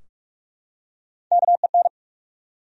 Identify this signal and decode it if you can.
Morse code, heard as a keyed tone, about 46 wpm, 700 Hz. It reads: KEN
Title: USA Common Names - Max 3 Letters - 45wpm